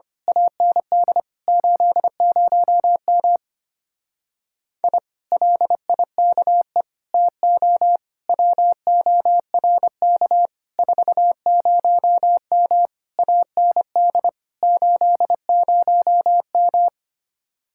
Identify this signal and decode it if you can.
Morse code, heard as a keyed tone, about 24 wpm, 700 Hz. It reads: AND 80M I LIKE TO WORK 40M AND 80M
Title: Speed-Racing Common QSO Phrases - 25wpm